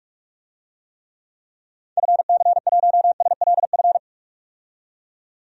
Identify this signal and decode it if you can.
Morse code, heard as a keyed tone, about 46 wpm, 700 Hz. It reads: FC1SLF
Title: DXCC-100 Call Signs - 6 Characters - 45wpm